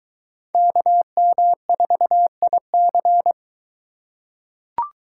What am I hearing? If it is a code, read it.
Morse code, heard as a keyed tone, about 24 wpm, 700 Hz. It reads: KM4IC